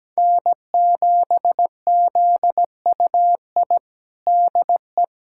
Morse code, heard as a keyed tone, about 17 wpm, 700 Hz. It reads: N7ZUI DE
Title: USA Callsign DE Callsign - 5 Characters - 17wpm